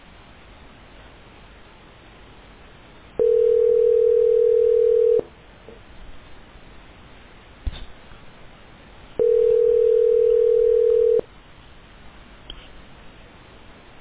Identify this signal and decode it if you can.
Ringback tone (North America), 440 + 480 Hz.